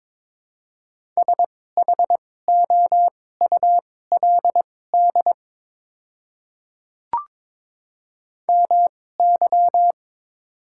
Morse code, heard as a keyed tone, about 22 wpm, spaced as for 16 wpm, 700 Hz. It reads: SHOULD MY
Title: 2x Character Spacing - Top 300 Words - 22wpm